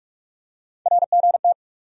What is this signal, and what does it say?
Morse code, heard as a keyed tone, about 42 wpm, 700 Hz. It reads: RGT
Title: USA Call Sign Postfixes - 3 Letters 45wpm